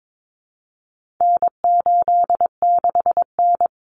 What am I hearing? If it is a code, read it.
Morse code, heard as a keyed tone, about 22 wpm, 700 Hz. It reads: N86N
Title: Common Cut Numbers - 4-Digit 22wpm